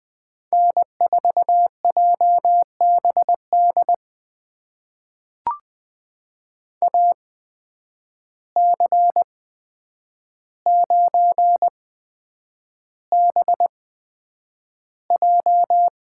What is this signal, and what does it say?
Morse code, heard as a keyed tone, about 20 wpm, spaced as for 9 wpm, 700 Hz. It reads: N4JBD AC9BJ
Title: USA Call Signs - 5 Characters - 8x Character Spacing - 20wpm